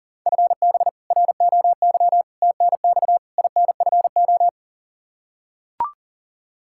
Morse code, heard as a keyed tone, about 40 wpm, 700 Hz. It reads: FB ROY TNX INFO
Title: K1USN SST QSO Practice 40wpm